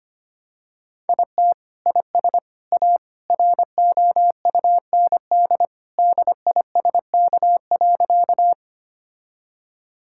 Morse code, heard as a keyed tone, about 25 wpm, 700 Hz. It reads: IT IS A ROUND DISK.
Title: Sentences from Top 1000 Words - Review Part 6 - 25wpm